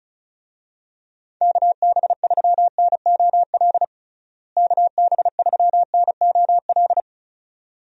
Morse code, heard as a keyed tone, about 34 wpm, 700 Hz. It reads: KB3NOL KB3NOL
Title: USA Call Signs - 6 Characters - Repeated 2x - 35wpm